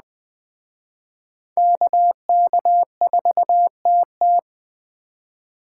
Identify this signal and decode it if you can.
Morse code, heard as a keyed tone, about 20 wpm, 700 Hz. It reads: KK4TT